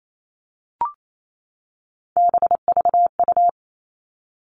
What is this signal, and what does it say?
Morse code, heard as a keyed tone, about 28 wpm, 700 Hz. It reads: BVU
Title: All Cut Numbers - 3-Digit 28wpm